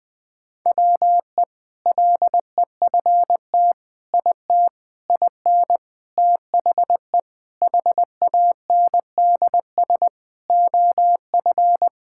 Morse code, heard as a keyed tone, about 20 wpm, 690 Hz. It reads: WE LEFT IT IN THE HANDS OF
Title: Sentences with 4-Word Phrases - Spoken Beginning - 20wpm